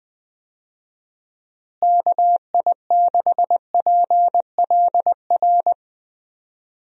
Morse code, heard as a keyed tone, about 19 wpm, 700 Hz. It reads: KI6PLR
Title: USA Call Signs - 6 Characters - Repeated 2x - 20wpm